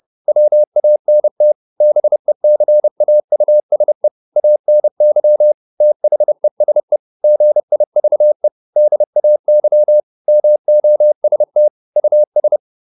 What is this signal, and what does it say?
Morse code, heard as a keyed tone, about 30 wpm, 600 Hz. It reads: WANT BECAUSE ANY THESE GIVE DAY MOST US
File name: Top-100-Words-30